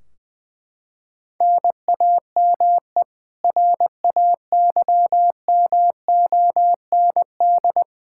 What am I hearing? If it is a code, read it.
Morse code, heard as a keyed tone, about 20 wpm, 700 Hz. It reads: NAME RAYMOND
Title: Practice QSO - Exchange 1 - Beginning - 20wpm